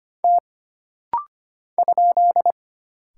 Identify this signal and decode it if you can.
Morse code, heard as a keyed tone, about 25 wpm, 700 Hz. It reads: T ?
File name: Single Character (Letter:Number:Punctuation:Prosign) - Rapid-Fire - 25wpm